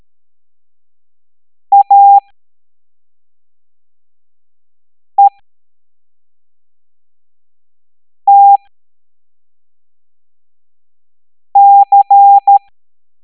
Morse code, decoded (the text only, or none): AETC